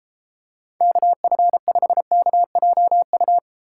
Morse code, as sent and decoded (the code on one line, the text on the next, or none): -.- ..-. ..... -.- .--- ..-
KF5KJU